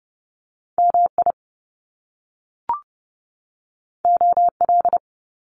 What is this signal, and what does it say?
Morse code, heard as a keyed tone, about 30 wpm, 700 Hz. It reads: MI OL
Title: Top Two Letter Combinations in Words 30wpm